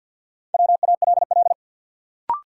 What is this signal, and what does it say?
Morse code, heard as a keyed tone, about 52 wpm, 700 Hz. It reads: WALL
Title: Sentences from Top 1100 Words - New Word Drill Alt - 50wpm